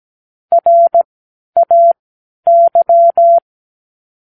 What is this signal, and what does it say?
Morse code, heard as a keyed tone, about 17 wpm, spaced as for 10 wpm, 680 Hz. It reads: RAY
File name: k7qo / Practicewords